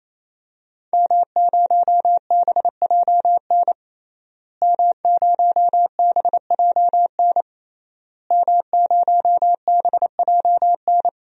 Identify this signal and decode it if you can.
Morse code, heard as a keyed tone, about 27 wpm, 700 Hz. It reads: M0BJN M0BJN M0BJN
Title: DXCC-100 Call Signs - 5 Characters - Repeated 3x - 28wpm